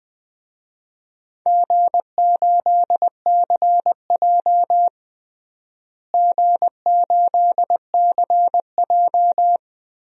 Morse code, heard as a keyed tone, about 20 wpm, 700 Hz. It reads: G8CJ G8CJ
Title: DXCC-100 Call Signs - 4 Characters - Repeated 3x - 20wpm